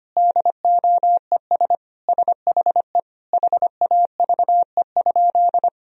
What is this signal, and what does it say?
Morse code, heard as a keyed tone, about 25 wpm, 700 Hz. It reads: DOES SHE HAVE?
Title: Sentences from Top 1000 Words - Review Part 1 - 25wpm